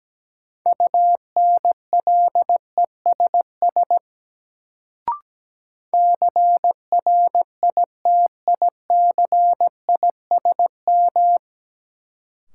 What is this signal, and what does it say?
Morse code, heard as a keyed tone, about 17 wpm, 700 Hz. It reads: UNLESS CRITICISM
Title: Sentences from Top 1300 Words - New Word Drill Alt - 17wpm